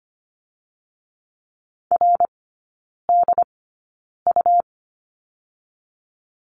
Morse code, heard as a keyed tone, about 25 wpm, 700 Hz. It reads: R D U
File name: Lesson 17e - Letter u - Three Characters 25wpm